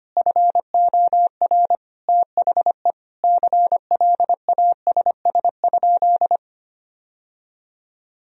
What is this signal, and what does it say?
Morse code, heard as a keyed tone, about 25 wpm, 700 Hz. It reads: FOR THE CLASS?
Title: Sentences from Top 1000 Words - Review Part 2 - 25wpm